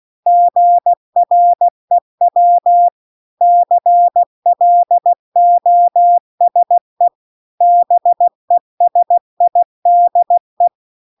Morse code, heard as a keyed tone, about 16 wpm, 700 Hz. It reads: GREW CLOSE BESIDE